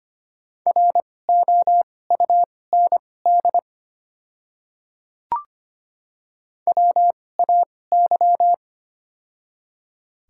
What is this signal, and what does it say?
Morse code, heard as a keyed tone, about 25 wpm, spaced as for 18 wpm, 700 Hz. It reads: ROUND WAY
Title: 2x Character Spacing - Top 200 Words - 25wpm